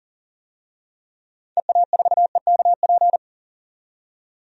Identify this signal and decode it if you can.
Morse code, heard as a keyed tone, about 40 wpm, 700 Hz. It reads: EA4EKP
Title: DXCC-100 Call Signs - 6 Characters - 40wpm